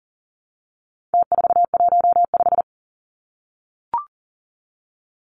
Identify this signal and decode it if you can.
Morse code, heard as a keyed tone, about 39 wpm, 700 Hz. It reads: T415
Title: Numbers - Contest Style - 4-Digit 40wpm